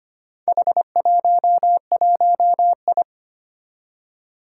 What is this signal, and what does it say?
Morse code, heard as a keyed tone, about 25 wpm, 700 Hz. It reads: H11I
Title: Four Challenging Letter-Number Combinations 25wpm